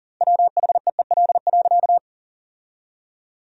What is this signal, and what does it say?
Morse code, heard as a keyed tone, about 40 wpm, 700 Hz. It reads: WHEEL.
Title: 2x Word-Spacing - Sentences from Top 500 Words - 40wpm